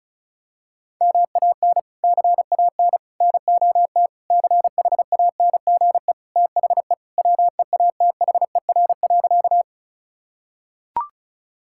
Morse code, heard as a keyed tone, about 36 wpm, 700 Hz. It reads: MAN CAN NOT CHANGE THE WEATHER.
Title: Sentences from Top 1000 Words - Review Part 5 - 35wpm